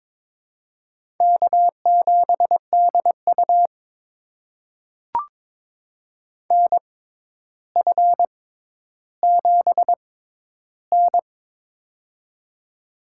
Morse code, heard as a keyed tone, about 22 wpm, spaced as for 11 wpm, 700 Hz. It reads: K7DU NF7N
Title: USA Call Signs - 4 Characters - 6x Character Spacing - 22wpm